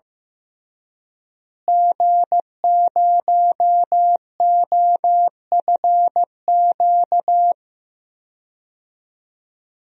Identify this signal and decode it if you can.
Morse code, heard as a keyed tone, about 15 wpm, 700 Hz. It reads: G0OFQ
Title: DXCC-100 Call Signs - 5 Characters - Repeated 2x - 15wpm